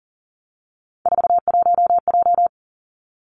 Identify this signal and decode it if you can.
Morse code, heard as a keyed tone, about 43 wpm, 700 Hz. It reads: 41J
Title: Three Challenging Letter-Number Combinations 40wpm